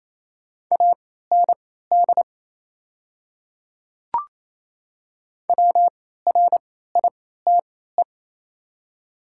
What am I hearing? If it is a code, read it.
Morse code, heard as a keyed tone, about 29 wpm, spaced as for 16 wpm, 700 Hz. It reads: AND WRITE
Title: 3x Character Spacing - Top 100 Words - 28wpm